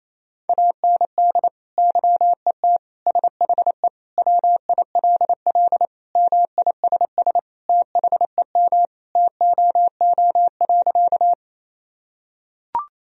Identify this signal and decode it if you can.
Morse code, heard as a keyed tone, about 28 wpm, 700 Hz. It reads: AND YET SHE WILL MISS THEM TOO.
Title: Sentences from Top 1000 Words - Review Part 3 - 28wpm